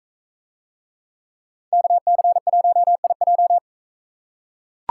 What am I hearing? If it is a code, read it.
Morse code, heard as a keyed tone, about 40 wpm, 700 Hz. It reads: KC1IJ